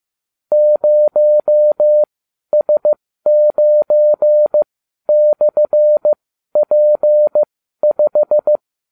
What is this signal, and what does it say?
Morse code, heard as a keyed tone, about 15 wpm, spaced as for 12 wpm, 600 Hz. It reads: 0S9/P5